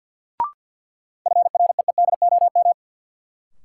Morse code, heard as a keyed tone, about 52 wpm, 700 Hz. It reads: FREEDOM
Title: Sentences from Top 800 Words - New Word Drill - 50wpm